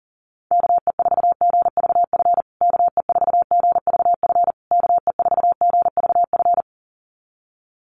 Morse code, heard as a keyed tone, about 40 wpm, 700 Hz. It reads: KE4GVF KE4GVF KE4GVF
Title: USA Call Signs Repeated 3x 40wpm